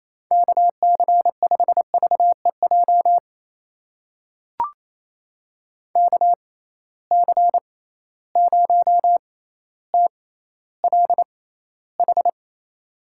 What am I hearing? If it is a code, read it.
Morse code, heard as a keyed tone, about 28 wpm, spaced as for 14 wpm, 700 Hz. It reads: KC5VEJ KC0TLH
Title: USA Call Signs - 6 Characters - 6x Character Spacing - 28wpm